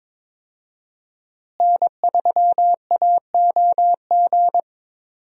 Morse code, heard as a keyed tone, about 22 wpm, 700 Hz. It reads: N3AOG
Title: USA Call Signs - 5 Characters - Repeated 2x - 22wpm